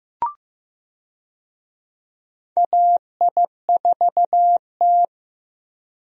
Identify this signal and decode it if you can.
Morse code, heard as a keyed tone, about 15 wpm, 700 Hz. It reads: AI4T
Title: USA Call Signs - 4 Characters - Repeated 3x - 15wpm